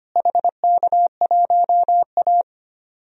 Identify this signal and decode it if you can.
Morse code, heard as a keyed tone, about 25 wpm, 700 Hz. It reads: HK1A
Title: DXCC-100 Call Signs - 4 Characters - Repeated 3x - 25wpm